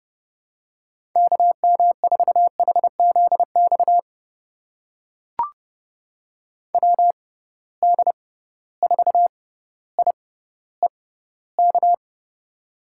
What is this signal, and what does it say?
Morse code, heard as a keyed tone, about 30 wpm, spaced as for 15 wpm, 700 Hz. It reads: KM4HZX WD4IEK
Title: USA Call Signs - 6 Characters - 6x Character Spacing - 30wpm